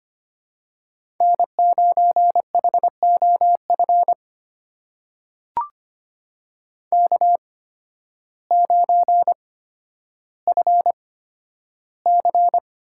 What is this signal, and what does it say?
Morse code, heard as a keyed tone, about 25 wpm, spaced as for 12 wpm, 700 Hz. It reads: N9HOF K9FC
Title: USA Call Signs - 5 Characters - 8x Character Spacing - 25wpm